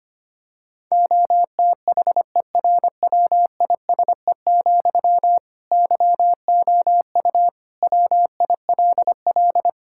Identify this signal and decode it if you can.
Morse code, heard as a keyed tone, about 25 wpm, 700 Hz. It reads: OTHERWISE, YOU WILL
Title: Sentences from Top 1600 Words 25wpm